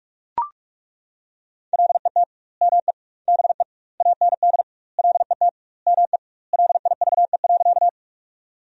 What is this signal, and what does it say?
Morse code, heard as a keyed tone, about 41 wpm, 700 Hz. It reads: LET ME BE AND LET ME LIVE.
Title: 2x Word-Spacing - Sentences from Top 300 Words - 45wpm